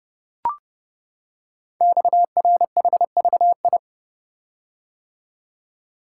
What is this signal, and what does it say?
Morse code, heard as a keyed tone, about 30 wpm, 690 Hz. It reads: XRHVI
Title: Code Groups - 5 Letters - 30wpm